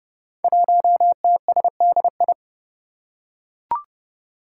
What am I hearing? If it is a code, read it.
Morse code, heard as a keyed tone, about 30 wpm, 700 Hz. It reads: 1TSDI
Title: Five Challenging Letter-Number Combinations 30wpm